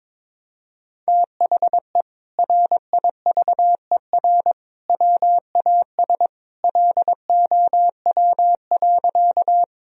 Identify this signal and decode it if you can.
Morse code, heard as a keyed tone, about 22 wpm, 700 Hz. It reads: THE RIVER WAS LOW.
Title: Sentences from Top 1000 Words - Review Part 1 - 22wpm